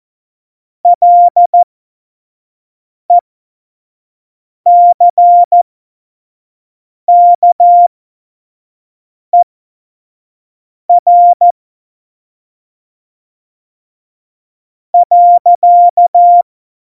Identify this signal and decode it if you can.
Morse code, heard as a keyed tone, about 14 wpm, spaced as for 5 wpm, 700 Hz. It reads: LECKER .
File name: Morsekurs / Witze / Witz 16. Oktober 2018, 25 F70